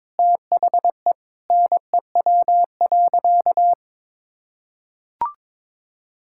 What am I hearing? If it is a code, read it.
Morse code, heard as a keyed tone, about 22 wpm, 700 Hz. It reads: THE NEW.
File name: practice / 22wpm / Sentences with 8-Word Phrases - 22wpm